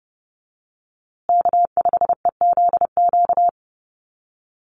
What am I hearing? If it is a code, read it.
Morse code, heard as a keyed tone, about 30 wpm, 700 Hz. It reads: K5EZQ
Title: USA Call Signs Repeated 2x 30wpm